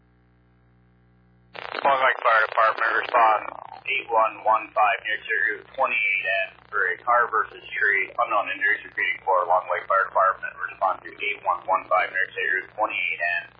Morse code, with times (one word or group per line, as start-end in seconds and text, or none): none